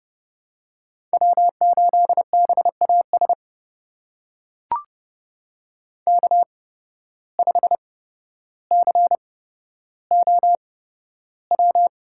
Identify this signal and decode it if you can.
Morse code, heard as a keyed tone, about 30 wpm, spaced as for 13 wpm, 700 Hz. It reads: W8BAS K5COW
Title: USA Call Signs - 5 Characters - 8x Character Spacing - 30wpm